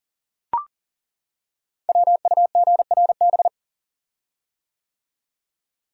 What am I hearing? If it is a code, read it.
Morse code, heard as a keyed tone, about 42 wpm, 700 Hz. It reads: WUGRB
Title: Code Groups - 5 Letters - 40wpm